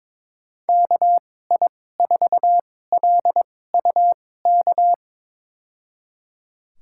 Morse code, heard as a keyed tone, about 22 wpm, spaced as for 16 wpm, 700 Hz. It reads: KI4LUK
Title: USA Call Signs - 6 Characters - 2x Character Spacing - 22wpm